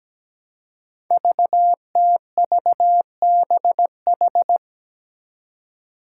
Morse code, heard as a keyed tone, about 17 wpm, 700 Hz. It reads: VTVBH
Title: Five Challenging Letter-Number Combinations 17wpm